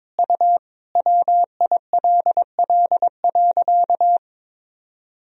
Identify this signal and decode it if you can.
Morse code, heard as a keyed tone, about 22 wpm, 700 Hz. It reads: U WILL.
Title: Sentences from Top 200 Words and Abbreviations 22wpm